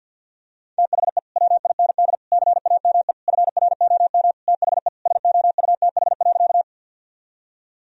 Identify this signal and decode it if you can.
Morse code, heard as a keyed tone, about 50 wpm, 700 Hz. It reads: THE WIND CAME FROM THE SOUTH.
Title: Sentences from Top 1000 Words - Review Part 2 - 50wpm